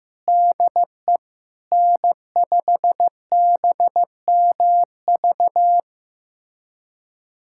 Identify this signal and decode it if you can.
Morse code, heard as a keyed tone, about 15 wpm, 700 Hz. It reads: DE N5BMV
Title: USA Callsign DE Callsign - 5 Characters - 15wpm